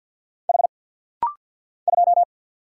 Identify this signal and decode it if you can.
Morse code, heard as a keyed tone, about 50 wpm, 700 Hz. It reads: H 2
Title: Single Letter-Number - Rapid-Fire 50wpm